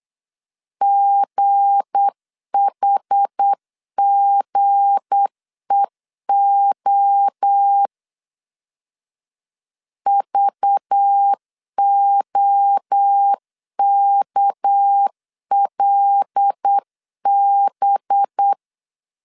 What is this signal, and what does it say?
Morse code, heard as a keyed tone, about 8 wpm, 800 Hz. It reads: GHGEO VOKLB
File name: LEZIONI / 7 GF9Z / 7 G 8wpm 3118